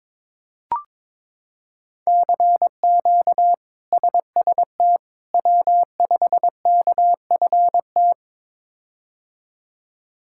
Morse code, heard as a keyed tone, about 22 wpm, 700 Hz. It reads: CQ SST W5KFT
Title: K1USN SST QSO Practice - 22wpm